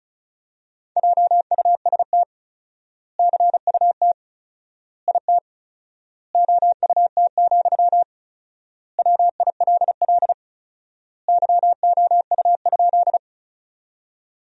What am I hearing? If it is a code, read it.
Morse code, heard as a keyed tone, about 34 wpm, 700 Hz. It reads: JUST CUT IT OUT, WILL YOU?
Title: 4x Word-Spacing - Sentences from Top 300 Words - 35wpm